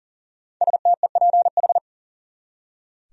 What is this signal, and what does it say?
Morse code, heard as a keyed tone, about 40 wpm, 700 Hz. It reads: STEPH
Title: USA Common Names - Max 5 Letters - 40wpm